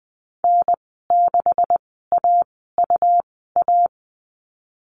Morse code, heard as a keyed tone, about 20 wpm, spaced as for 14 wpm, 700 Hz. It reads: N6AUA
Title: 147 Lesson 24g - Letter y - Call Signs 20wpm